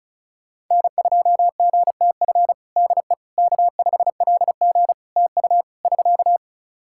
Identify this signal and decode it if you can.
Morse code, heard as a keyed tone, about 36 wpm, 700 Hz. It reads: N2GTF DE K5LG TU <SK>
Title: Practice QSO - Exchange 4 - Ending - 35wpm